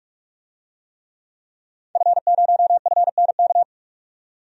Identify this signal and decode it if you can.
Morse code, heard as a keyed tone, about 42 wpm, 700 Hz. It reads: F0FNK
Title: DXCC-100 Call Signs - 5 Characters - 45wpm